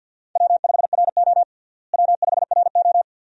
Morse code, heard as a keyed tone, about 53 wpm, 700 Hz. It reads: W5RO W5RO
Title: USA Call Signs - 4 Characters - Repeated 2x - 50wpm